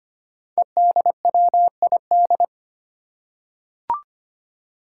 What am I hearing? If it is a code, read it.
Morse code, heard as a keyed tone, about 25 wpm, 700 Hz. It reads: EDWID
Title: Five Challenging Letter-Number Combinations 25wpm